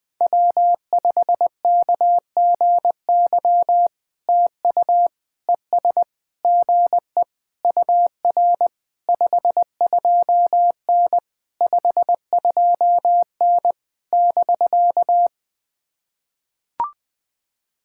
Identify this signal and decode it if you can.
Morse code, heard as a keyed tone, about 19 wpm, 700 Hz. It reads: W5KGY TU ES GE UR 52N 52N <BK>
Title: POTA QSO Practice - 20wpm